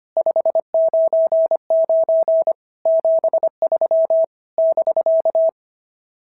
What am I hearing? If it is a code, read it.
Morse code, heard as a keyed tone, about 24 wpm, 650 Hz. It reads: 599 73 <BK>
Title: SOTA QSO Practice 25wpm